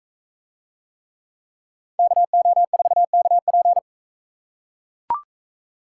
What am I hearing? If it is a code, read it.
Morse code, heard as a keyed tone, about 44 wpm, 700 Hz. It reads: KO4KP